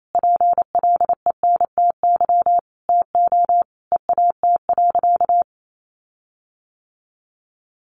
Morse code, heard as a keyed tone, about 28 wpm, 700 Hz. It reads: PLENTY TO EAT.
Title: Sentences from Top 1300 Words 28wpm